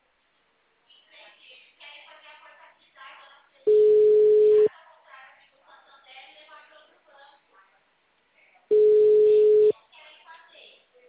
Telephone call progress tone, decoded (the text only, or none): ringback tone